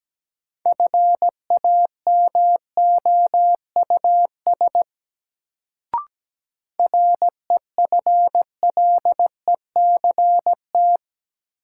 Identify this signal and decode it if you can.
Morse code, heard as a keyed tone, about 17 wpm, 700 Hz. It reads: FAMOUS REFLECT
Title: Sentences from Top 1500 Words - New Word Drill - 17wpm